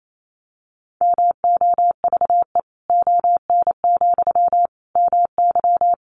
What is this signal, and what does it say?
Morse code, heard as a keyed tone, about 28 wpm, 700 Hz. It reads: MOVE ON, MY
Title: Sentences from Top 500 Words 28wpm